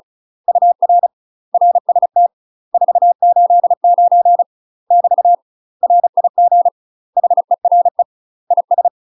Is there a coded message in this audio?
Yes: Morse code from 0.5 to 8.9 s, heard as a keyed tone, about 36 wpm, 700 Hz. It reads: UR RST 489 <BT> RIG HERE IS